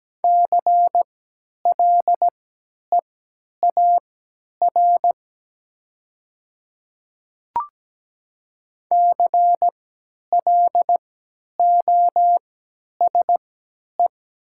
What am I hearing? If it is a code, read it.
Morse code, heard as a keyed tone, about 17 wpm, spaced as for 10 wpm, 690 Hz. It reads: CLEAR CLOSE
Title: 3x Character Spacing - Top 500 Words - 17wpm